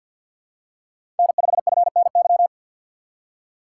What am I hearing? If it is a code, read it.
Morse code, heard as a keyed tone, about 51 wpm, 700 Hz. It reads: N5FNY
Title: USA Call Signs - 5 Characters - 6x Character Spacing - 50wpm